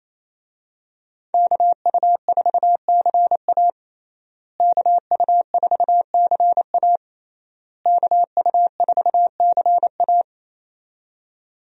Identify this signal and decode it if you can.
Morse code, heard as a keyed tone, about 27 wpm, 700 Hz. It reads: KU4CA KU4CA KU4CA
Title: USA Call Signs - 5 Characters - Repeated 3x - 28wpm